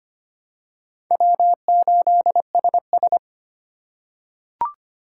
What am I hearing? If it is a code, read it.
Morse code, heard as a keyed tone, about 25 wpm, 700 Hz. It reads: W8SS
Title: USA Call Signs - 4 Characters - 3x Character Spacing - 25wpm